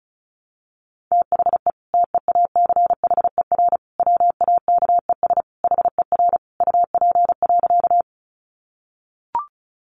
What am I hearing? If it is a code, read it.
Morse code, heard as a keyed tone, about 34 wpm, 700 Hz. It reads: THE TEACHER WAKES HER UP.
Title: Easy Phrases 35wpm